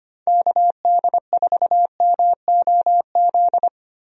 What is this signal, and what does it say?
Morse code, heard as a keyed tone, about 25 wpm, 700 Hz. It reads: KD4MOZ
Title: USA Call Signs - 6 Characters - 25wpm